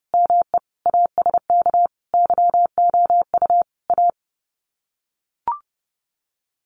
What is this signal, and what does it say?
Morse code, heard as a keyed tone, about 30 wpm, 700 Hz. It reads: ME ASK YOU A
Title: Sets of 5 Words Encore 30wpm